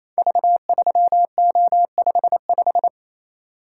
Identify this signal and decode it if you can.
Morse code, heard as a keyed tone, about 28 wpm, 700 Hz. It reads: V3O55
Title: Code Groups - 5 Letter:Numbers - 28wpm